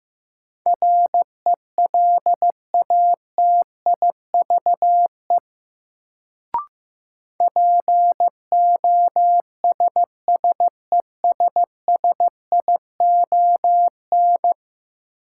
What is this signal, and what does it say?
Morse code, heard as a keyed tone, about 15 wpm, 700 Hz. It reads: RELATIVE POSSESSION
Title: Sentences from Top 1400 Words - New Word Drill Alt - 15wpm